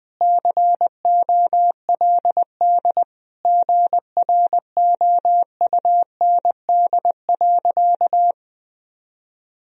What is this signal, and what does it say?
Morse code, heard as a keyed tone, about 20 wpm, 700 Hz. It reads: COLD GROUND.
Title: Sentences from Top 1000 Words - Review Part 3 - 20wpm